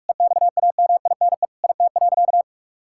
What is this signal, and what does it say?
Morse code, heard as a keyed tone, about 46 wpm, 700 Hz. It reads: EXAMINE IT.
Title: Sentences from Top 1600 Words 45wpm